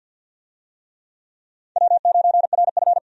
Morse code, heard as a keyed tone, about 48 wpm, 700 Hz. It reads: W9RF
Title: USA Call Signs - 4 Characters - 50wpm